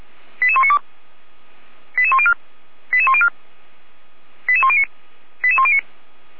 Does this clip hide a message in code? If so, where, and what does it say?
Yes: five-tone selective call (ZVEI) from 0.4 to 5.8 s, heard as quick runs of five tones, tones of about 70 ms. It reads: 88192 88104 88104 88109 88109